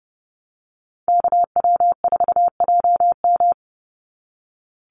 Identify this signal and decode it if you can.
Morse code, heard as a keyed tone, about 30 wpm, 700 Hz. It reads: KW4JM
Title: NA SOTA Activator Callsigns 30wpm